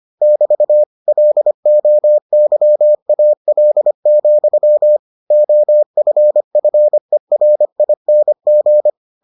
Morse code, heard as a keyed tone, about 25 wpm, 600 Hz. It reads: <BT> LOYAL, OFFERING